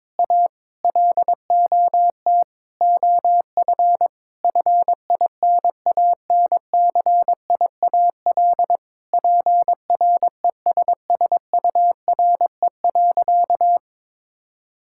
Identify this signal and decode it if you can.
Morse code, heard as a keyed tone, about 22 wpm, 700 Hz. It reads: A LOT OF FINANCIAL PRESSURE.
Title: Sentences from Top 1000 Words - Review Part 6 - 22wpm